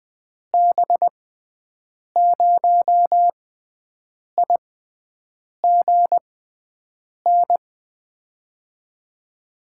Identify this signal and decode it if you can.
Morse code, heard as a keyed tone, about 20 wpm, spaced as for 7 wpm, 700 Hz. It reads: B0IGN
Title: USA Call Signs - 6 Characters - 6x Character Spacing - 20wpm